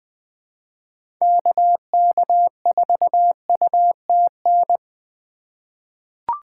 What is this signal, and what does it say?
Morse code, heard as a keyed tone, about 20 wpm, 700 Hz. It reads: KK4UTN
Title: USA Call Signs - 6 Characters - 3x Character Spacing - 20wpm